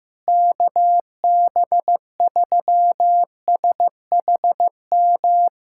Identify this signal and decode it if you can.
Morse code, heard as a keyed tone, about 15 wpm, 700 Hz. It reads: KB3SHM
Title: USA Callsign DE Callsign - 6 Characters - 15wpm